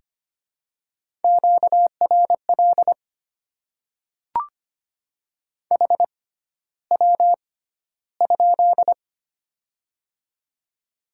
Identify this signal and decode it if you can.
Morse code, heard as a keyed tone, about 25 wpm, spaced as for 13 wpm, 690 Hz. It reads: QRL HW?
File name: Top 20 QSO Elements - 6x Character Spacing - 25wpm